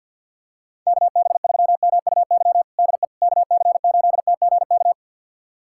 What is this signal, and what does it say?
Morse code, heard as a keyed tone, about 52 wpm, 700 Hz. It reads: KB3MUY DE KC8TGK
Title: USA Callsign DE Callsign - 6 Characters - 50wpm